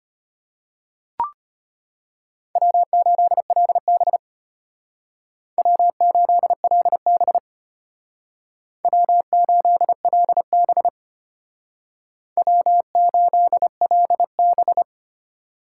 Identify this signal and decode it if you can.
Morse code, heard as a keyed tone, about 32 wpm, 700 Hz. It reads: W8LB W8LB W8LB W8LB